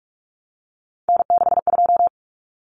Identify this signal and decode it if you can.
Morse code, heard as a keyed tone, about 44 wpm, 700 Hz. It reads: N62
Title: Common Cut Numbers - 3-Digit 45wpm